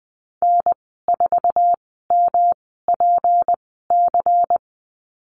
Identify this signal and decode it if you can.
Morse code, heard as a keyed tone, about 20 wpm, spaced as for 15 wpm, 700 Hz. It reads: N4MPC
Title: 179 Lesson 29g - Slash - Call Signs 20wpm